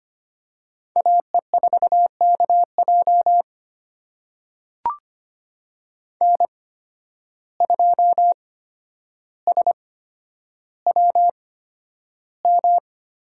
Morse code, heard as a keyed tone, about 25 wpm, spaced as for 11 wpm, 700 Hz. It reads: AE4KJ N2SWM
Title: USA Call Signs - 5 Characters - 8x Character Spacing - 25wpm